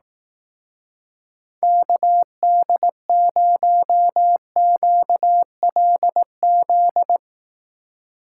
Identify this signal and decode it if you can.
Morse code, heard as a keyed tone, about 18 wpm, 700 Hz. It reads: KD0QLZ
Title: Speed Racing - USA 2x3 Call Signs 15wpm